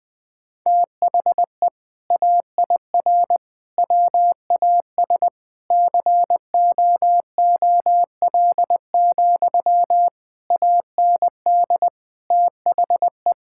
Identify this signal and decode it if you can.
Morse code, heard as a keyed tone, about 21 wpm, 700 Hz. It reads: THE AIR WAS COOL, AND THE
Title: Sentences from Top 1600 Words 20wpm